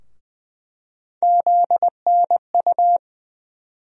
Morse code, heard as a keyed tone, about 20 wpm, 700 Hz. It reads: ZNU
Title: USA Call Sign Postfixes - 3 Letters 20wpm